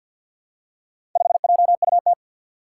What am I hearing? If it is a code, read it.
Morse code, heard as a keyed tone, about 50 wpm, 700 Hz. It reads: 5JUT
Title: Four Challenging Letter-Number Combinations 50wpm